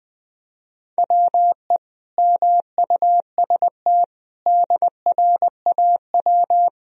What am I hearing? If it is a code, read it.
Morse code, heard as a keyed tone, about 21 wpm, 700 Hz. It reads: WE MUST DRAW
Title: Sentences from Top 1000 Words - Review Part 1 - 20wpm